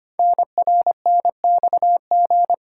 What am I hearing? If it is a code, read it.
Morse code, heard as a keyed tone, about 25 wpm, 700 Hz. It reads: NRNXG
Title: Code Groups - 5 Letters - 25wpm